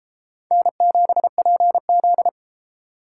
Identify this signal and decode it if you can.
Morse code, heard as a keyed tone, about 31 wpm, 700 Hz. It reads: N7PZ